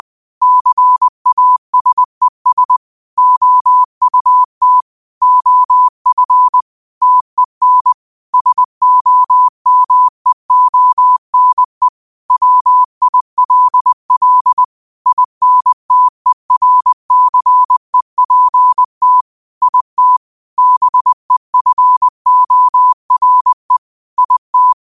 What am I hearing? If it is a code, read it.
Morse code, heard as a keyed tone, about 20 wpm, 1000 Hz. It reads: CASES OUT OF TEN SOMEONE WILL INTERCEPT IT BEFORE IT
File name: qotdfiles / 20wpm